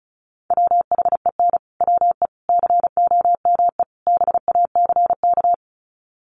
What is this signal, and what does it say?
Morse code, heard as a keyed tone, about 34 wpm, 700 Hz. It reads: WHEN WE COME BACK
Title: Sets of 4 Words 35wpm